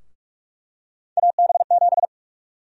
Morse code, heard as a keyed tone, about 47 wpm, 700 Hz. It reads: AB7
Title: USA Call Sign Prefixes - Repeated 2x - 45wpm